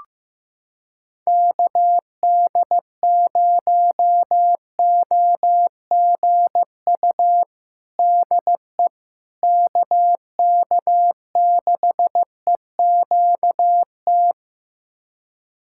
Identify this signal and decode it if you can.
Morse code, heard as a keyed tone, about 15 wpm, 700 Hz. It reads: KD0OGU DE KK6EQT
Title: USA Callsign DE Callsign - 6 Characters - 15wpm